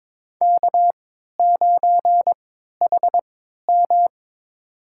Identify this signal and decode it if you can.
Morse code, heard as a keyed tone, about 21 wpm, spaced as for 12 wpm, 690 Hz. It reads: K9HM
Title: USA Call Signs - 4 Characters - 3x Character Spacing - 22wpm